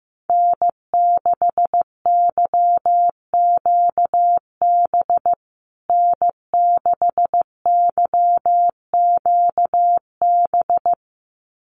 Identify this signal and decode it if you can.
Morse code, heard as a keyed tone, about 15 wpm, 700 Hz. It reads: N6YQB N6YQB
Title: USA Call Signs Repeated 2x 15wpm